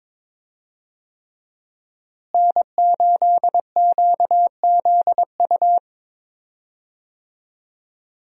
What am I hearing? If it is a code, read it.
Morse code, heard as a keyed tone, about 22 wpm, 700 Hz. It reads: N8QZU
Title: USA Call Signs - 5 Characters - Repeated 3x - 22wpm